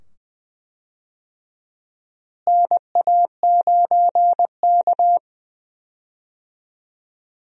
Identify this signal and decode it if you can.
Morse code, heard as a keyed tone, about 21 wpm, 700 Hz. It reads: NA9K
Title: USA Call Signs - 4 Characters - Repeated 2x - 20wpm